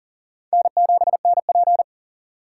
Morse code, heard as a keyed tone, about 40 wpm, 700 Hz. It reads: N7NP
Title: USA Call Signs - 3x Character Spacing - 40wpm